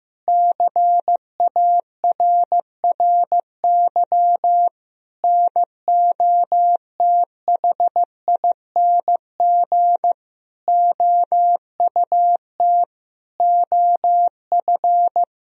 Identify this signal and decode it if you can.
Morse code, heard as a keyed tone, about 15 wpm, 700 Hz. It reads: CARRY NOTHING OUT OF
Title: Sentences from Top 1000 Words - Review Part 1 - 15wpm